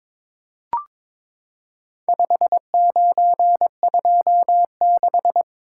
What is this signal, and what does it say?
Morse code, heard as a keyed tone, about 23 wpm, 700 Hz. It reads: 5926
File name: practice / 22wpm / Four Letter-Number Combinations - 22wpm